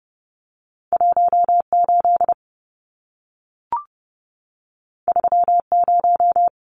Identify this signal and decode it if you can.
Morse code, heard as a keyed tone, about 30 wpm, 700 Hz. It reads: 18 30
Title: Numbers - 2-Digit 30wpm